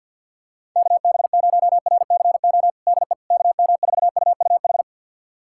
Morse code, heard as a keyed tone, about 51 wpm, 690 Hz. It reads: KB0RCO DE KM4UUH
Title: USA Callsign DE Callsign - 6 Characters - 50wpm